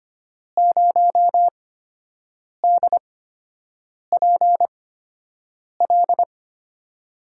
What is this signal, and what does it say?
Morse code, heard as a keyed tone, about 25 wpm, spaced as for 7 wpm, 700 Hz. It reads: 0DPL
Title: USA Call Signs - 8x Character Spacing - 25wpm